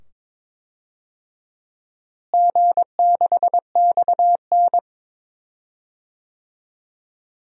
Morse code, heard as a keyed tone, about 22 wpm, 700 Hz. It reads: G6XN
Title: DXCC-100 Call Signs - 4 Characters - Repeated 2x - 22wpm